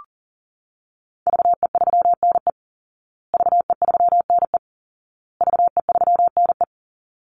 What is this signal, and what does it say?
Morse code, heard as a keyed tone, about 40 wpm, 700 Hz. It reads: VE3NE VE3NE VE3NE
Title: Call Signs Repeated 2x - 40wpm